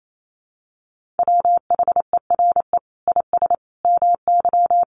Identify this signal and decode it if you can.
Morse code, heard as a keyed tone, about 28 wpm, 700 Hz. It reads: WHERE IS MY
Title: Sentences from Top 1400 Words 28wpm